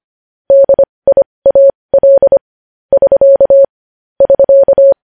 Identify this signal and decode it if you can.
Morse code, heard as a keyed tone, about 25 wpm, spaced as for 20 wpm, 550 Hz. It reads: DIAL <SK> <SK>
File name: podcast / News headlines-1x-25x20